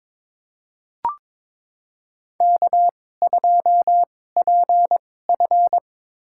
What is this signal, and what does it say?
Morse code, heard as a keyed tone, about 23 wpm, spaced as for 16 wpm, 700 Hz. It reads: K2PF
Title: USA Call Signs - 4 Characters - 2x Character Spacing - 22wpm